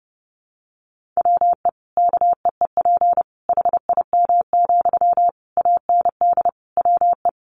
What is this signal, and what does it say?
Morse code, heard as a keyed tone, about 30 wpm, 700 Hz. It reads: WE KEEP HIM, AND WE
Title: Sentences from Top 600 Words 30wpm